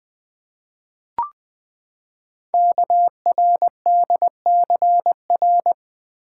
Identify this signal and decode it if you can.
Morse code, heard as a keyed tone, about 20 wpm, 700 Hz. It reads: KRDCR
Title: Code Groups - 5 Letters - 20wpm